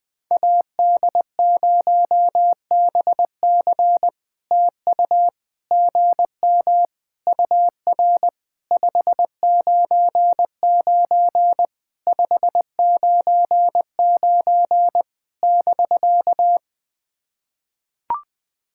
Morse code, heard as a keyed tone, about 20 wpm, 700 Hz. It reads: AD0BC TU GM UR 599 599 <BK>
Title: POTA QSO Practice - 20wpm